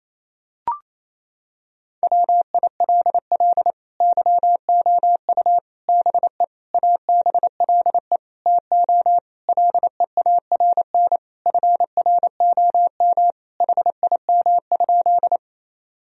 Morse code, heard as a keyed tone, about 28 wpm, 700 Hz. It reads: WILL YOU BE ABLE TO LEARN FROM HIM?
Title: Sentences from Top 1000 Words - Review Part 2 - 28wpm